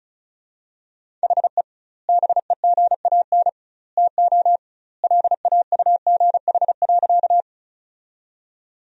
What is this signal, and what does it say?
Morse code, heard as a keyed tone, about 36 wpm, 700 Hz. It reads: HE BEGAN TO LAUGH.
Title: 2x Word-Spacing - Sentences from Top 500 Words - 35wpm